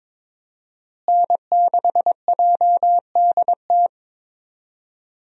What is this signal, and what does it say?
Morse code, heard as a keyed tone, about 22 wpm, 700 Hz. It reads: N6JDT
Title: USA Call Signs - 5 Characters - Repeated 3x - 22wpm